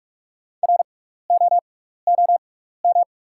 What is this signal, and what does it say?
Morse code, heard as a keyed tone, about 48 wpm, spaced as for 16 wpm, 690 Hz. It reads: ROOM
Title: 6x Character Spacing - Top 300 Words - 45wpm